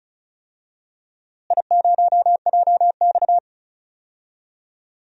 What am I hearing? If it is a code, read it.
Morse code, heard as a keyed tone, about 34 wpm, 700 Hz. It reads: I0JX
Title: DXCC-100 Call Signs - 4 Characters - Repeated 3x - 35wpm